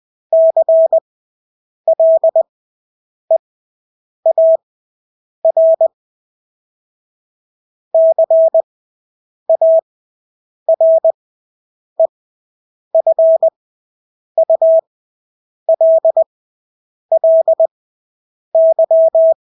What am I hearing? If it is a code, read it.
Morse code, heard as a keyed tone, about 20 wpm, spaced as for 8 wpm, 650 Hz. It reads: CLEAR CAREFULLY
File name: Engelske ord 20/8 wpm